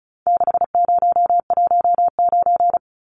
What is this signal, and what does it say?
Morse code, heard as a keyed tone, about 34 wpm, 700 Hz. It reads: 6019
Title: Numbers - 4-Digit 35wpm